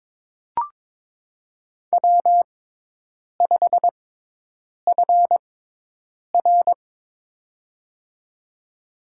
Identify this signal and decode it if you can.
Morse code, heard as a keyed tone, about 22 wpm, spaced as for 8 wpm, 700 Hz. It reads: W5FR
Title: USA Call Signs - 6x Character Spacing - 22wpm